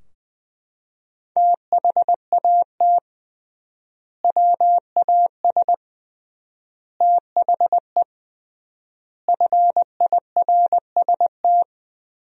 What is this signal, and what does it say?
Morse code, heard as a keyed tone, about 21 wpm, 700 Hz. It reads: THAT WAS THE FIRST
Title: 3x Word-Spacing - Sentences from Top 300 Words - 20wpm